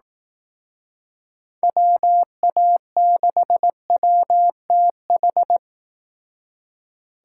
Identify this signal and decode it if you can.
Morse code, heard as a keyed tone, about 19 wpm, 700 Hz. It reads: WA6WTH
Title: Speed Racing - USA Call Signs 15wpm